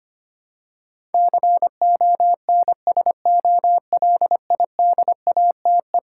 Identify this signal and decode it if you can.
Morse code, heard as a keyed tone, about 25 wpm, 700 Hz. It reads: CONSOLIDATE
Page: morse-code-ninja-practice-25wpm